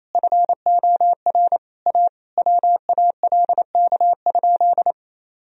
Morse code, heard as a keyed tone, about 28 wpm, 700 Hz. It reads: FOR A WALK?
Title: Sentences from Top 1000 Words - Review Part 1 - 28wpm